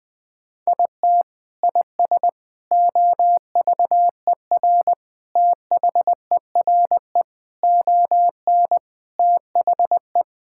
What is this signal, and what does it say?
Morse code, heard as a keyed tone, about 20 wpm, 700 Hz. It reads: IT IS OVER THERE ON THE
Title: Sentences from Top 1000 Words - Review Part 6 - 20wpm